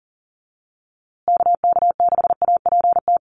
Morse code, heard as a keyed tone, about 40 wpm, 700 Hz. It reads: KK6APT
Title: USA Call Signs Repeated 2x 40wpm